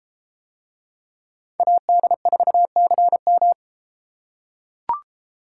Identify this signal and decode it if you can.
Morse code, heard as a keyed tone, about 34 wpm, 700 Hz. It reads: AD4CM